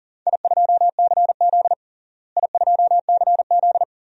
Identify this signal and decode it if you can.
Morse code, heard as a keyed tone, about 40 wpm, 700 Hz. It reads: I2CZ I2CZ
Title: DXCC-100 Call Signs - 4 Characters - Repeated 3x - 40wpm